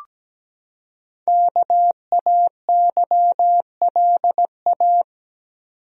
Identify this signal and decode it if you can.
Morse code, heard as a keyed tone, about 17 wpm, 700 Hz. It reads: KAYLA
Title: USA Common Names - Max 5 Letters - 17wpm